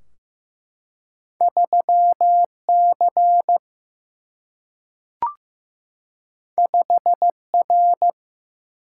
Morse code, heard as a keyed tone, about 15 wpm, 700 Hz. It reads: 3C 5R
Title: Single-Letter Single-Number 15wpm